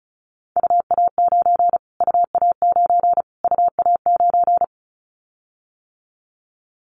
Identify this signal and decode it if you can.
Morse code, heard as a keyed tone, about 34 wpm, 700 Hz. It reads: UA9 UA9 UA9
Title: Callsign Prefixes for Top 25 DXCC - Repeated 3x - 35wpm